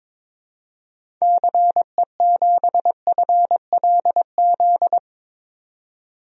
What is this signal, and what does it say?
Morse code, heard as a keyed tone, about 22 wpm, 700 Hz. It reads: CE7FLZ
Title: DXCC-100 Call Signs - 6 Characters - 22wpm